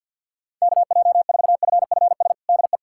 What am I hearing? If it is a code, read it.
Morse code, heard as a keyed tone, about 52 wpm, 700 Hz. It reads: KJ4FFS DE
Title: USA Callsign DE Callsign - 6 Characters - 50wpm